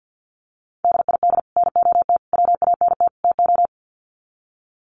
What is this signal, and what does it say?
Morse code, heard as a keyed tone, about 50 wpm, 700 Hz. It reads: DID NOT WANT TO
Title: Sets of 4 Words 50wpm